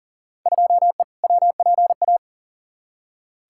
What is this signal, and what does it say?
Morse code, heard as a keyed tone, about 40 wpm, 700 Hz. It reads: 2E WPA
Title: Field Day Exchanges Encore - 40wpm